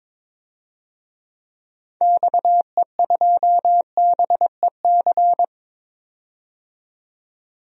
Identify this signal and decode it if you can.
Morse code, heard as a keyed tone, about 21 wpm, 700 Hz. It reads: XE2BEC
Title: DXCC-100 Call Signs - 6 Characters - 22wpm